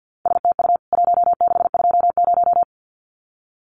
Morse code, heard as a keyed tone, about 50 wpm, 700 Hz. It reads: STU 1620
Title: CWT Exchange 50wpm